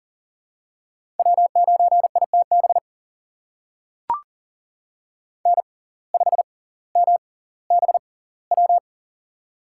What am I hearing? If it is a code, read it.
Morse code, heard as a keyed tone, about 40 wpm, spaced as for 20 wpm, 700 Hz. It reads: W9ITB N5MBW
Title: USA Call Signs - 5 Characters - 6x Character Spacing - 40wpm